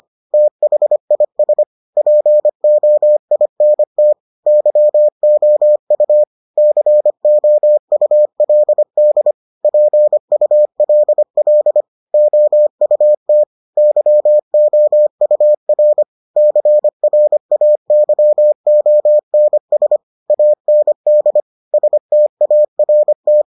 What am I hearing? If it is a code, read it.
Morse code, heard as a keyed tone, about 25 wpm, 600 Hz. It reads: THIS POINT YOU COULD PULL OUT YOUR CRAYONS AND START